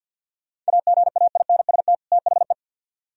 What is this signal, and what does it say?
Morse code, heard as a keyed tone, about 65 wpm, 700 Hz. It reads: AGAINST THE